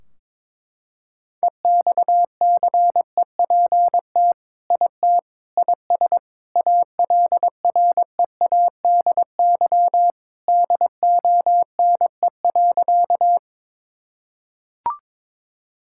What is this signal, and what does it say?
Morse code, heard as a keyed tone, about 22 wpm, 700 Hz. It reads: EXCEPT IT IS ALREADY DONE.
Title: Sentences from Top 1600 Words 22wpm